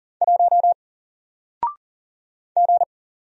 Morse code, heard as a keyed tone, about 40 wpm, 700 Hz. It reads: 1 G
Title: Single Character (Letter:Number:Punctuation:Prosign) Alt 40wpm